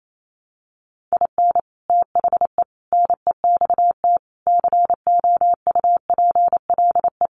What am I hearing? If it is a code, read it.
Morse code, heard as a keyed tone, about 28 wpm, 700 Hz. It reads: IN THE NEXT COUPLE
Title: Sets of 5 Words Encore2 28wpm